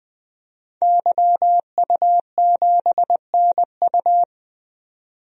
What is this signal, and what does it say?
Morse code, heard as a keyed tone, about 20 wpm, 700 Hz. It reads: YU7NU
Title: DXCC-100 Call Signs - 5 Characters - 20wpm